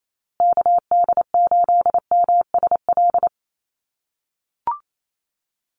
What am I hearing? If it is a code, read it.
Morse code, heard as a keyed tone, about 28 wpm, 700 Hz. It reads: KD8MSL
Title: USA Call Signs Repeated 3x 28wpm